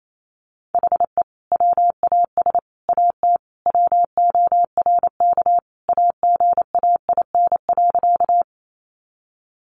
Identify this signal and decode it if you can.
Morse code, heard as a keyed tone, about 28 wpm, 700 Hz. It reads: HE WAS AT WORK AGAIN.
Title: Sentences from Top 200 Words 28wpm